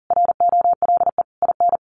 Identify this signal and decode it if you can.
Morse code, heard as a keyed tone, about 40 wpm, 700 Hz. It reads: ROLE IN
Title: Sets of 5 Words Encore2 40wpm